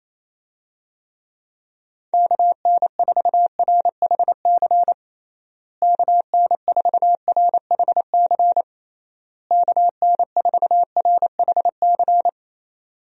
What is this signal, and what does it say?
Morse code, heard as a keyed tone, about 27 wpm, 700 Hz. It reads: KN4RHC KN4RHC KN4RHC
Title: USA Call Signs - 6 Characters - Repeated 3x - 28wpm